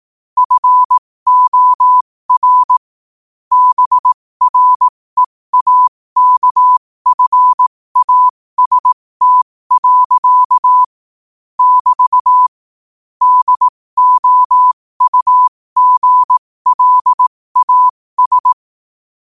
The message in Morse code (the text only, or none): FOR BREAKFAST. <BT> DOUGLAS